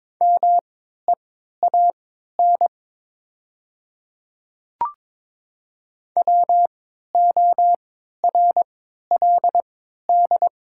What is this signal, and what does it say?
Morse code, heard as a keyed tone, about 22 wpm, spaced as for 13 wpm, 700 Hz. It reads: MEAN WORLD
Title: 3x Character Spacing - Top 200 Words - 22wpm